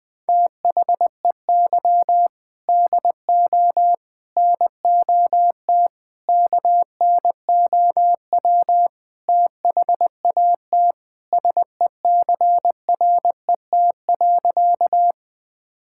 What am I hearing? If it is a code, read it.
Morse code, heard as a keyed tone, about 20 wpm, 700 Hz. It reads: THEY DO NOT KNOW THAT SECRET.
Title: Sentences with 4-Word Phrases - 20wpm